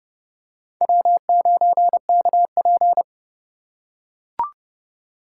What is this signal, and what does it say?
Morse code, heard as a keyed tone, about 31 wpm, 700 Hz. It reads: W9KP
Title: USA Call Signs - 4 Characters - 6x Character Spacing - 30wpm